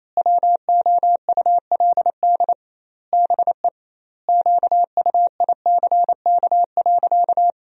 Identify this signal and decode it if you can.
Morse code, heard as a keyed tone, about 28 wpm, 700 Hz. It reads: WOULD BE QUICK.
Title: 2x Word-Spacing - Sentences from Top 500 Words - 28wpm